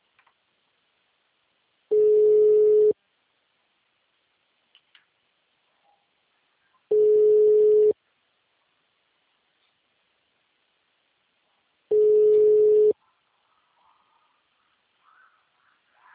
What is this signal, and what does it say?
Ringback tone (Europe, 425 Hz).
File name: out- -2011- - - .219